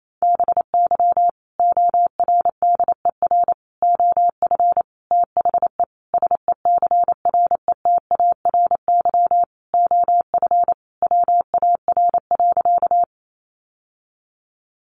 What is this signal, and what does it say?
Morse code, heard as a keyed tone, about 28 wpm, 700 Hz. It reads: BY ORDER OF THE SECRETARY OF WAR.
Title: Sentences from Top 1300 Words 28wpm